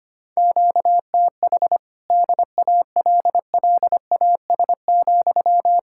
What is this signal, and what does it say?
Morse code, heard as a keyed tone, about 33 wpm, 700 Hz. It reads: QTH DALLAS,